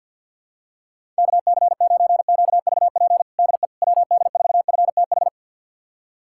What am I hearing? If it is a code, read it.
Morse code, heard as a keyed tone, about 51 wpm, 700 Hz. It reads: KC9QVP DE WD4FTH
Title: USA Callsign DE Callsign - 6 Characters - 50wpm